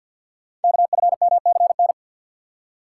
Morse code, heard as a keyed tone, about 50 wpm, 690 Hz. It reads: KFMCN
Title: Code Groups - 5 Letters - 50wpm